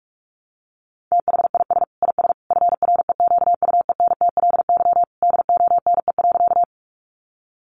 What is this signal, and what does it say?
Morse code, heard as a keyed tone, about 44 wpm, 700 Hz. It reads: THIS IS FREQUENTLY DONE.
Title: Sentences from Top 1300 Words 45wpm